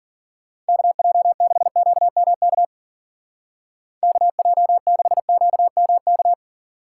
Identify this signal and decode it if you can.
Morse code, heard as a keyed tone, about 48 wpm, 700 Hz. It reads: KJ6QMK KJ6QMK
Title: Speed Racing - USA 2x3 Call Signs 40wpm